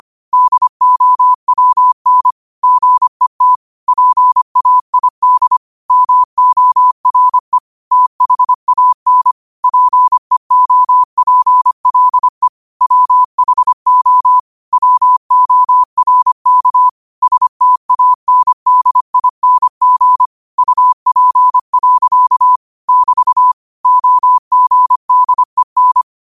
Morse code, heard as a keyed tone, about 25 wpm, 1000 Hz. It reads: DOWN GET PAID MORE THAN PEOPLE WHO WORK STANDING UP. <BT> OGDEN
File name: qotdfiles / 25wpm